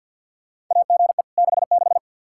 Morse code, heard as a keyed tone, about 47 wpm, 700 Hz. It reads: AGE 66
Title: K1USN SST QSO Practice 50wpm